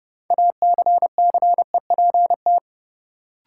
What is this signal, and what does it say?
Morse code, heard as a keyed tone, about 31 wpm, 700 Hz. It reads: ACCEPT
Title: Sentences from Top 1300 Words - New Word Drill Alt - 30wpm